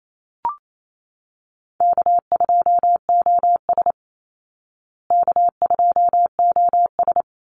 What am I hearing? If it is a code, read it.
Morse code, heard as a keyed tone, about 28 wpm, 700 Hz. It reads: K2OS K2OS
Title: Easy Call Signs Repeated 3x - 28wpm